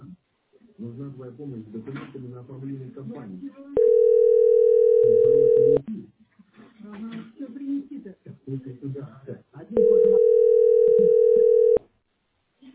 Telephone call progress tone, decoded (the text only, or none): ringback tone